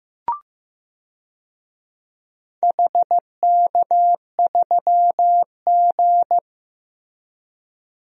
Morse code, heard as a keyed tone, about 15 wpm, 700 Hz. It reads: HK3G
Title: DXCC-100 Call Signs - 4 Characters - Repeated 3x - 15wpm